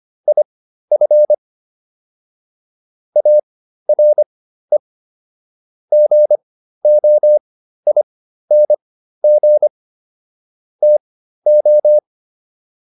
Morse code, heard as a keyed tone, about 25 wpm, spaced as for 13 wpm, 600 Hz. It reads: IF ARE GOING TO
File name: WD404 13